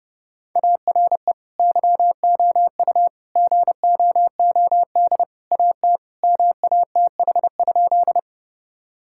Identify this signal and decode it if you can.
Morse code, heard as a keyed tone, about 30 wpm, 700 Hz. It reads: ARE YOU GOOD AT MATH?
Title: Sentences from Top 1000 Words - Review Part 6 - 30wpm